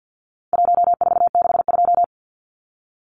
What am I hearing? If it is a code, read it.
Morse code, heard as a keyed tone, about 50 wpm, 700 Hz. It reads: 1462
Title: Numbers - 4-Digit 50wpm